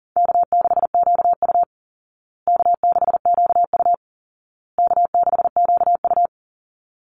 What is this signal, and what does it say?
Morse code, heard as a keyed tone, about 40 wpm, 700 Hz. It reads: K6QU K6QU K6QU
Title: Call Signs Repeated 2x - 40wpm